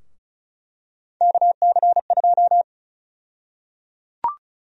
Morse code, heard as a keyed tone, about 34 wpm, 700 Hz. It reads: KC2
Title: USA Call Sign Prefixes - Repeated 2x - 35wpm